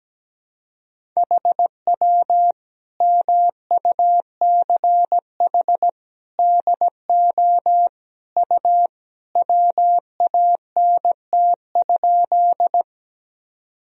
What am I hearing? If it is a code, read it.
Morse code, heard as a keyed tone, about 17 wpm, 700 Hz. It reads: HW MUCH DO U WANT?
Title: Sentences from Top 200 Words and Abbreviations 17wpm